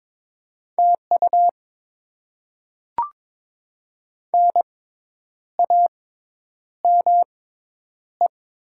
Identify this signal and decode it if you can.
Morse code, heard as a keyed tone, about 22 wpm, 700 Hz. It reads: T U N A M E
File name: Top 20 QSO Elements - 6x Character Spacing - 22wpm